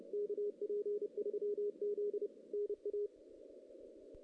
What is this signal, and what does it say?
Morse code, heard as a keyed tone, about 30 wpm, 410 Hz. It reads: KP3Z NA
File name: 2016 JAN CW NAQP / 2016 Jan naqp k5ot - 0-5